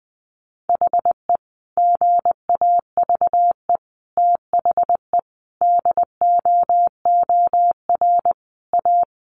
Morse code, heard as a keyed tone, about 20 wpm, 700 Hz. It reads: HE GAVE THE DOOR A
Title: Sentences from Top 1300 Words 20wpm